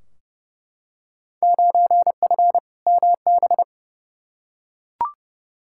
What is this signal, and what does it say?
Morse code, heard as a keyed tone, about 30 wpm, 700 Hz. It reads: 9F MB
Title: Field Day Exchanges Encore - 30wpm